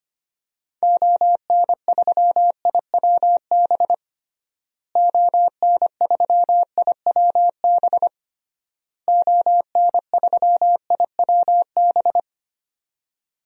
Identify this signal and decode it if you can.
Morse code, heard as a keyed tone, about 25 wpm, 700 Hz. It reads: ON3IWB ON3IWB ON3IWB
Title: DXCC-100 Call Signs - 6 Characters - Repeated 3x - 25wpm